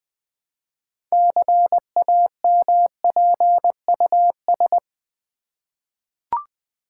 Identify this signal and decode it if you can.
Morse code, heard as a keyed tone, about 20 wpm, 700 Hz. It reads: CAMPUS
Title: 6 Letter Words - First 3 Letters Given - 20wpm